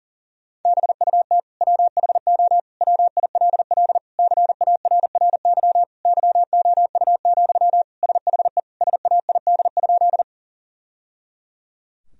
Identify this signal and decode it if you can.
Morse code, heard as a keyed tone, about 40 wpm, 700 Hz. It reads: BUT WHO WILL CARRY YOU, SHE SAID?
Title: Sentences from Top 1000 Words - Review Part 1 - 40wpm